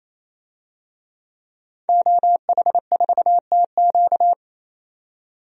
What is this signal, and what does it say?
Morse code, heard as a keyed tone, about 28 wpm, 700 Hz. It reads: OH4TQ
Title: DXCC-100 Call Signs - 5 Characters - 28wpm